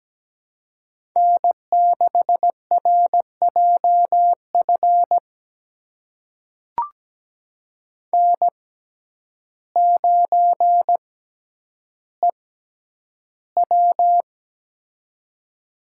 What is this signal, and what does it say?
Morse code, heard as a keyed tone, about 17 wpm, spaced as for 9 wpm, 700 Hz. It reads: N6RJF N9EW
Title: USA Call Signs - 6x Character Spacing - 17wpm